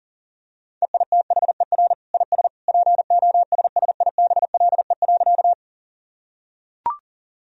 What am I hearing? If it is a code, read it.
Morse code, heard as a keyed tone, about 40 wpm, 700 Hz. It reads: EITHER IS POSSIBLE.
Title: Sentences from Top 1000 Words - Review Part 4 - 40wpm